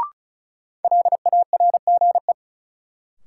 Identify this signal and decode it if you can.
Morse code, heard as a keyed tone, about 36 wpm, 700 Hz. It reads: LARGE
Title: Sets of 4 Words - New Word Drill - 35wpm